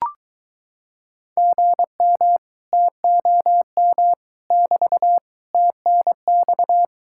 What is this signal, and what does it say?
Morse code, heard as a keyed tone, about 23 wpm, 700 Hz. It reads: GM TOM <BT> TNX